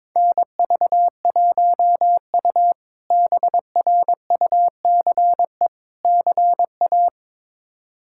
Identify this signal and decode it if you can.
Morse code, heard as a keyed tone, about 22 wpm, 700 Hz. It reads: NV1U BRUCE CA
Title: K1USN SST QSO Practice - 22wpm